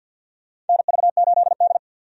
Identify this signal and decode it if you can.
Morse code, heard as a keyed tone, about 52 wpm, 700 Hz. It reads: NV8D
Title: USA Call Signs - 4 Characters - 50wpm